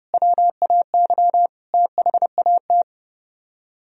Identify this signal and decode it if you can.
Morse code, heard as a keyed tone, about 36 wpm, 700 Hz. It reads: WAY THAT
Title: Speed-Racing - Sets of 5 Words 30wpm